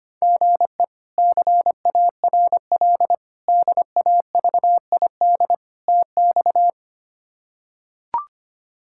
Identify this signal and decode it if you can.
Morse code, heard as a keyed tone, about 25 wpm, 700 Hz. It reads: GE CARL DAVID TX
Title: K1USN SST QSO Practice - 25wpm